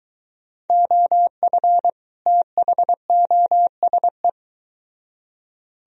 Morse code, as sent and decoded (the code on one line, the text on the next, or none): --- ..-. / - .... --- ... .
OF THOSE